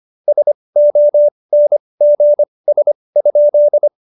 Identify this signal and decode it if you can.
Morse code, heard as a keyed tone, about 25 wpm, spaced as for 20 wpm, 590 Hz. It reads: SONGS?
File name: Jokes-25-char-20-eff